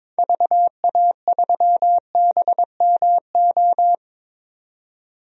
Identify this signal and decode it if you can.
Morse code, heard as a keyed tone, about 22 wpm, 700 Hz. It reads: VA3BMO
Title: DXCC-100 Call Signs - 6 Characters - 22wpm